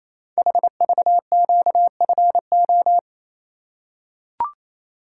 Morse code, heard as a keyed tone, about 28 wpm, 700 Hz. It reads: HVQFO
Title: Code Groups - 5 Letters - 28wpm